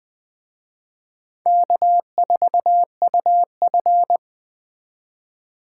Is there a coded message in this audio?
Yes: Morse code from 1.5 to 4.2 s, heard as a keyed tone, about 20 wpm, 700 Hz. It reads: K4UF